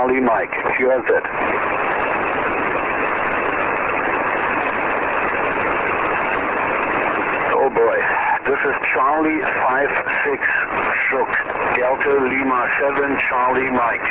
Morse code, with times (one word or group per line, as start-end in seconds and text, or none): none